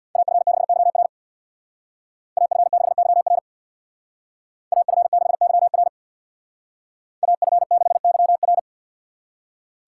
Morse code, heard as a keyed tone, about 65 wpm, 700 Hz. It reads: AF6YR AF6YR AF6YR AF6YR